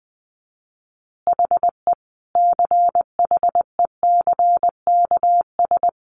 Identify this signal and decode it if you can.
Morse code, heard as a keyed tone, about 20 wpm, 700 Hz. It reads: HE CHECKS
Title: Easy Phrases 20wpm